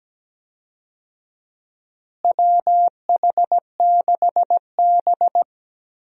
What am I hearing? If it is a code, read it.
Morse code, heard as a keyed tone, about 17 wpm, 700 Hz. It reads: WH6B
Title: USA Call Signs - 4 Characters - Repeated 2x - 17wpm